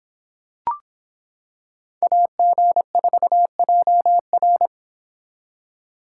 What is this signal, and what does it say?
Morse code, heard as a keyed tone, about 25 wpm, 700 Hz. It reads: AG4JR